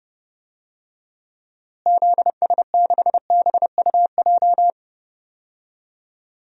Morse code, heard as a keyed tone, about 32 wpm, 700 Hz. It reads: ZS6BUJ
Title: DXCC-100 Call Signs - 6 Characters - Repeated 3x - 30wpm